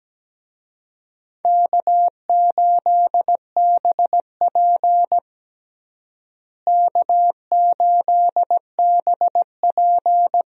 Morse code, heard as a keyed tone, about 17 wpm, 700 Hz. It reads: K8BP K8BP
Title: USA Call Signs - 4 Characters - Repeated 3x - 17wpm